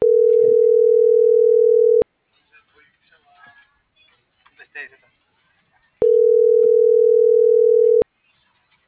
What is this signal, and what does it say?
Ringback tone (North America), 440 + 480 Hz.